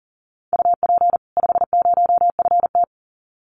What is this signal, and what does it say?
Morse code, heard as a keyed tone, about 40 wpm, 700 Hz. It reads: UP 50FT
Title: Common QSO Phrases 40wpm